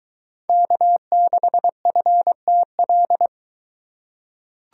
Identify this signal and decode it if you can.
Morse code, heard as a keyed tone, about 22 wpm, 700 Hz. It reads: K6FTL